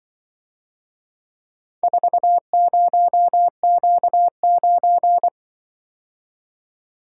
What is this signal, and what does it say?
Morse code, heard as a keyed tone, about 23 wpm, 700 Hz. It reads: 40Q9
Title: Speed-Racing - Four Letter-Number Combinations - 20wpm